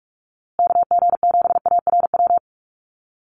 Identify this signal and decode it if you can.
Morse code, heard as a keyed tone, about 44 wpm, 700 Hz. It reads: KG7ARW
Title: USA Call Signs Repeated 3x 45wpm